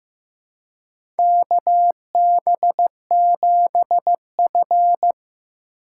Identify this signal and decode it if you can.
Morse code, heard as a keyed tone, about 15 wpm, 700 Hz. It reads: KB7F
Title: USA Call Signs - 4 Characters - 4x Character Spacing - 15wpm